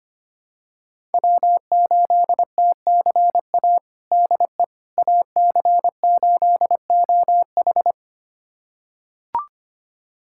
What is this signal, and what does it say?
Morse code, heard as a keyed tone, about 25 wpm, 700 Hz. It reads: W8TCA DE AC8OH
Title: USA Callsign DE Callsign - 5 Characters - 25wpm